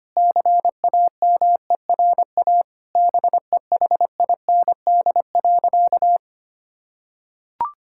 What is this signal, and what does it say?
Morse code, heard as a keyed tone, about 25 wpm, 700 Hz. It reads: CAMERA BEHIND.
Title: Sentences from Top 1000 Words - Review Part 5 - 25wpm